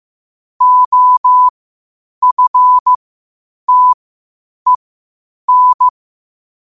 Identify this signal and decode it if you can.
Morse code, heard as a keyed tone, about 15 wpm, spaced as for 8 wpm, 1000 Hz. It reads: OFTEN
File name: qotdfiles / 8wpm